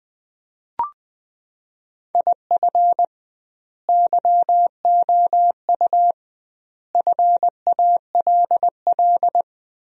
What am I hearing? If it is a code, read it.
Morse code, heard as a keyed tone, about 20 wpm, 700 Hz. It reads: IF YOU FALL
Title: 2x Word-Spacing - Sentences from Top 500 Words - 20wpm